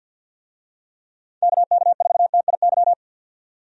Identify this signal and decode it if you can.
Morse code, heard as a keyed tone, about 51 wpm, 700 Hz. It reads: KK4TIY
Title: USA Call Signs - 6 Characters - 50wpm